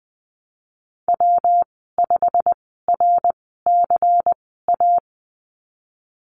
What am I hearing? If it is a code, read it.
Morse code, heard as a keyed tone, about 20 wpm, spaced as for 15 wpm, 700 Hz. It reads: W5RCA
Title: 186 Lesson 30g - Letter b - Call Signs 20wpm